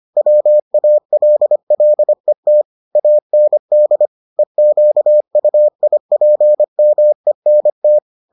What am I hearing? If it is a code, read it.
Morse code, heard as a keyed tone, about 25 wpm, 600 Hz. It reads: WALLET AND EQUIPMENT